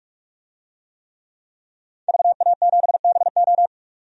Morse code, heard as a keyed tone, about 46 wpm, 700 Hz. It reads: VA7BO
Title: DXCC-100 Call Signs - 5 Characters - 45wpm